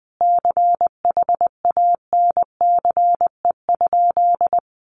Morse code, heard as a keyed tone, about 20 wpm, 700 Hz. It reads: CHANCE?
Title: Sentences from Top 800 Words 20wpm